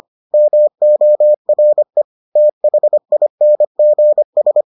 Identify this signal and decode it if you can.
Morse code, heard as a keyed tone, about 25 wpm, 600 Hz. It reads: MORE THINGS